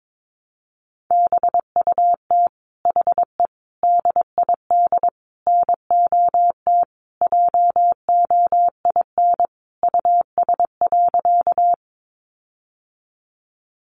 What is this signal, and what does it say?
Morse code, heard as a keyed tone, about 22 wpm, 700 Hz. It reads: BUT HE DID NOT JOIN US.
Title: Sentences from Top 1400 Words 22wpm